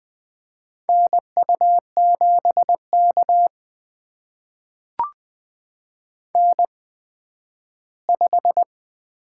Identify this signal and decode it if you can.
Morse code, heard as a keyed tone, about 20 wpm, spaced as for 12 wpm, 700 Hz. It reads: NU7K N5
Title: USA Call Signs - 8x Character Spacing - 20wpm